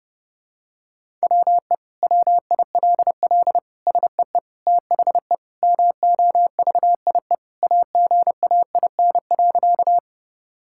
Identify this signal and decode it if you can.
Morse code, heard as a keyed tone, about 30 wpm, 700 Hz. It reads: WE WILL SEE THE MOVIE AGAIN.
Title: Sentences from Top 1000 Words - Review Part 5 - 30wpm